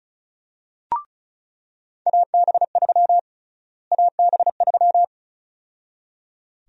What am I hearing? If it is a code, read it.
Morse code, heard as a keyed tone, about 36 wpm, 700 Hz. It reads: AB3 AB3
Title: USA Call Sign Prefixes - Repeated 2x - 35wpm